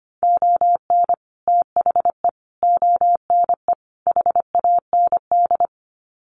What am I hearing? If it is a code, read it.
Morse code, heard as a keyed tone, about 25 wpm, 700 Hz. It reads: ON THE ONE HAND
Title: Sets of 4 Words 25wpm